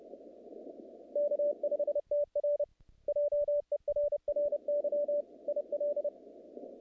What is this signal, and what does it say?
Morse code, heard as a keyed tone, about 29 wpm, 580 Hz. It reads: K5TR JERRY IL
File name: contests / naqp-cw-k5tr-op-k5pi- -2200